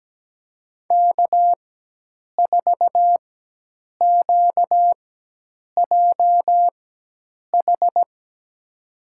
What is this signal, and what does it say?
Morse code, heard as a keyed tone, about 17 wpm, spaced as for 8 wpm, 700 Hz. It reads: K4QJH
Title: USA Call Signs - 5 Characters - 4x Character Spacing - 17wpm